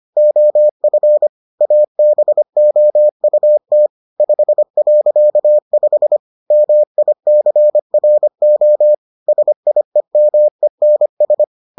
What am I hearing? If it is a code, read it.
Morse code, heard as a keyed tone, about 25 wpm, 600 Hz. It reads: OF ABOUT 5.5 MICRO SIEMENS